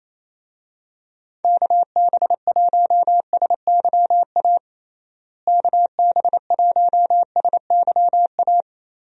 Morse code, heard as a keyed tone, about 28 wpm, 700 Hz. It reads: KB1SYA KB1SYA
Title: USA Call Signs - 6 Characters - Repeated 2x - 28wpm